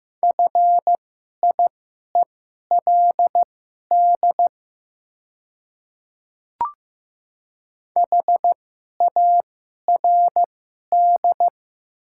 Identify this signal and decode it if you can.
Morse code, heard as a keyed tone, about 15 wpm, spaced as for 11 wpm, 700 Hz. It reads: FIELD HARD
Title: 2x Character Spacing - Top 500 Words - 15wpm